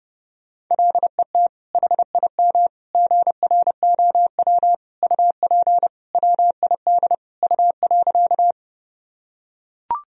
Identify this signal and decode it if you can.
Morse code, heard as a keyed tone, about 30 wpm, 700 Hz. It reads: LET HIM GROW UP WID U.
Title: Sentences from Top 300 Words and Abbreviations 30wpm